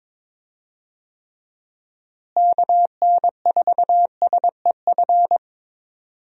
Morse code, heard as a keyed tone, about 22 wpm, 700 Hz. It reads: KN4SEF
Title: USA Call Signs - 6 Characters - Repeated 2x - 22wpm